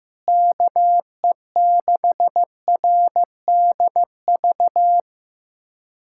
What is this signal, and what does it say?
Morse code, heard as a keyed tone, about 16 wpm, 700 Hz. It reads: KE6RDV
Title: USA Callsign DE Callsign - 6 Characters - 15wpm